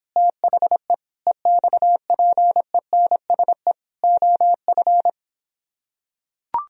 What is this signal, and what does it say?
Morse code, heard as a keyed tone, about 26 wpm, 700 Hz. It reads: THE EXPENSE OF